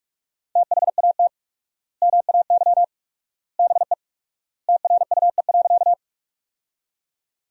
Morse code, heard as a keyed tone, about 46 wpm, 700 Hz. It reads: THAT MAY BE TRUE.
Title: 4x Word-Spacing - Sentences from Top 400 Words - 45wpm